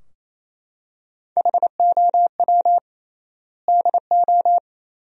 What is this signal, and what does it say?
Morse code, heard as a keyed tone, about 28 wpm, 700 Hz. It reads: HOW DO
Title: 3x Word-Spacing - Sentences from Top 100 Words - 28wpm